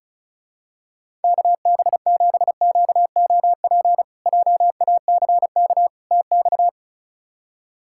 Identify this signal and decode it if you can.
Morse code, heard as a keyed tone, about 36 wpm, 700 Hz. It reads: KB7QOP JACK TX
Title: K1USN SST QSO Practice - 35wpm